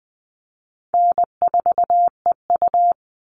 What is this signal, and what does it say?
Morse code, heard as a keyed tone, about 20 wpm, 700 Hz. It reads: N4EU
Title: All Cut Numbers - 4-Digit 20wpm